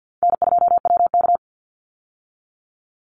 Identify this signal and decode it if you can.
Morse code, heard as a keyed tone, about 50 wpm, 700 Hz. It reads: N2WK
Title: Easy Call Signs - 50wpm